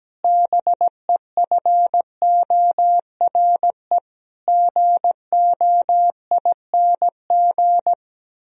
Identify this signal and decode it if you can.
Morse code, heard as a keyed tone, about 17 wpm, 700 Hz. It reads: BEFORE GOING
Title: Sentences from Top 1000 Words - Review Part 3 - 17wpm